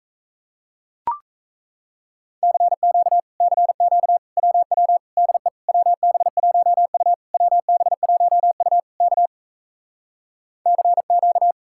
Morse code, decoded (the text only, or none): CQ CQ WW DE WB1U WB1U K CQ